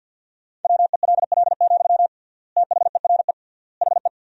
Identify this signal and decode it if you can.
Morse code, heard as a keyed tone, about 47 wpm, 700 Hz. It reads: WELL, THERE HE
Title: 3x Word-Spacing - Sentences from Top 500 Words - 50wpm